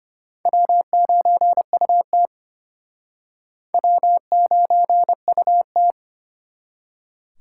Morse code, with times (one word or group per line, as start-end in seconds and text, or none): 0.5-2.3 W9UT
3.7-5.9 W9UT